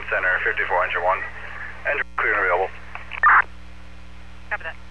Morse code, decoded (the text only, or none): none